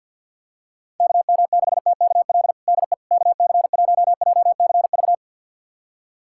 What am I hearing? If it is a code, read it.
Morse code, heard as a keyed tone, about 45 wpm, 700 Hz. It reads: KM6TKL DE KC1JCV
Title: USA Callsign DE Callsign 50wpm